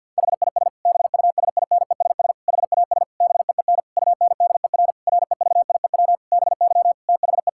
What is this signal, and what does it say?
Morse code, heard as a keyed tone, about 50 wpm, 700 Hz. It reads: HIS BUSINESS HAS BEEN UNDER REVIEW BY THE
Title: Sentences from Top 1000 Words - Review Part 6 - 50wpm